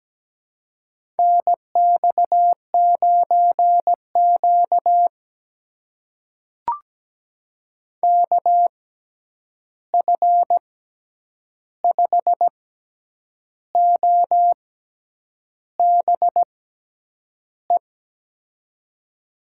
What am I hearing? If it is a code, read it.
Morse code, heard as a keyed tone, about 17 wpm, spaced as for 8 wpm, 700 Hz. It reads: NX9Q KF5OBE
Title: USA Call Signs - 6x Character Spacing - 17wpm